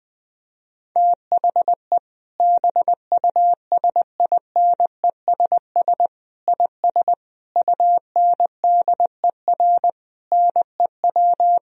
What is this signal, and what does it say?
Morse code, heard as a keyed tone, about 19 wpm, 700 Hz. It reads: THE BUSINESS IS UNDER NEW
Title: Sentences from Top 1000 Words - Review Part 4 - 20wpm